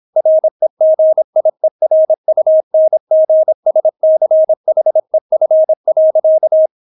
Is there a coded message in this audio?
Yes: Morse code from 0.2 to 6.7 s, heard as a keyed tone, about 26 wpm, 630 Hz. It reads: REGIERUNGSCHEF.